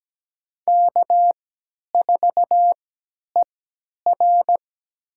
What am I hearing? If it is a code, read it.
Morse code, heard as a keyed tone, about 17 wpm, spaced as for 10 wpm, 700 Hz. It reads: K4ER
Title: USA Call Signs - 3x Character Spacing - 17wpm